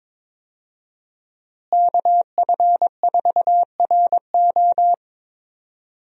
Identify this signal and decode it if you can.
Morse code, heard as a keyed tone, about 22 wpm, 700 Hz. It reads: KF4RO
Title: USA Call Signs - 5 Characters - Repeated 2x - 22wpm